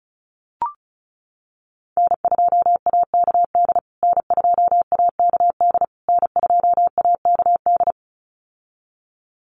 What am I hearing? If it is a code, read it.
Morse code, heard as a keyed tone, about 34 wpm, 700 Hz. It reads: N2AKD N2AKD N2AKD
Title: USA Call Signs Repeated 3x 35wpm